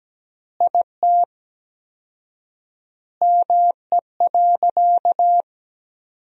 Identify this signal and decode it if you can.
Morse code, heard as a keyed tone, about 17 wpm, 700 Hz. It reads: IT ME.
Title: 4x Word-Spacing - Sentences from Top 200 Words - 17wpm